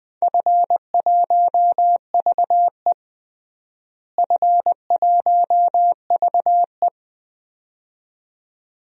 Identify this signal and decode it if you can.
Morse code, heard as a keyed tone, about 21 wpm, 700 Hz. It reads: F1VE F1VE
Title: DXCC-100 Call Signs - 4 Characters - Repeated 2x - 20wpm